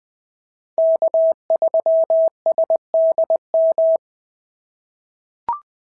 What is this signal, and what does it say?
Morse code, heard as a keyed tone, about 20 wpm, 650 Hz. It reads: K3SDM
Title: POTA QSO Practice - 20wpm